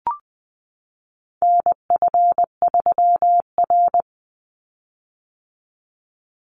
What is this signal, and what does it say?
Morse code, heard as a keyed tone, about 20 wpm, 700 Hz. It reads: NF3R
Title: Call Signs - 20wpm